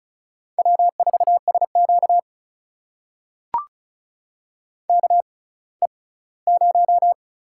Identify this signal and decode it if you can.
Morse code, heard as a keyed tone, about 34 wpm, spaced as for 20 wpm, 700 Hz. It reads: W4SQ KE0
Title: USA Call Signs - 6x Character Spacing - 35wpm